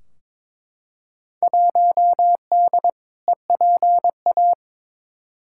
Morse code, heard as a keyed tone, about 22 wpm, 700 Hz. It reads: 1D EPA
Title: Field Day Exchanges - 22wpm